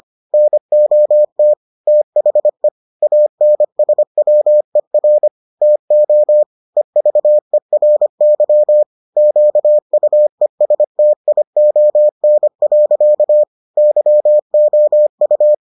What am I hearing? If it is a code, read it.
Morse code, heard as a keyed tone, about 25 wpm, 600 Hz. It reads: NOT THE ANSWER TO EVERY QUESTION. YOU